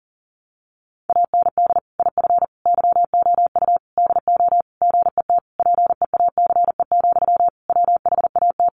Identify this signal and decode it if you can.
Morse code, heard as a keyed tone, about 40 wpm, 700 Hz. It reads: AND IF YOU DO GET PEACE, WHAT
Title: Sentences from Top 1500 Words 40wpm